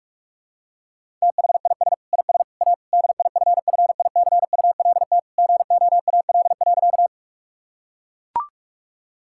Morse code, heard as a keyed tone, about 42 wpm, 700 Hz. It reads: THIS IS A DIFFICULT GOAL.
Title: Sentences from Top 1000 Words - Review Part 5 - 45wpm